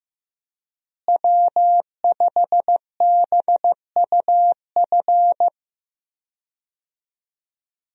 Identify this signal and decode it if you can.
Morse code, heard as a keyed tone, about 15 wpm, 700 Hz. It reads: W5BUF
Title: USA Call Signs - 5 Characters - Repeated 3x - 15wpm